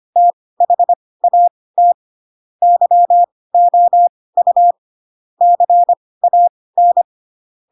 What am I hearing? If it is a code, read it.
Morse code, heard as a keyed tone, about 25 wpm, spaced as for 18 wpm, 700 Hz. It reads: THAT YOU CAN
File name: Pinocchio / Pinocchio ch4 12 25 ik1hge